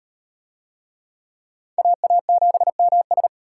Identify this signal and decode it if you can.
Morse code, heard as a keyed tone, about 37 wpm, 700 Hz. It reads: AA7MS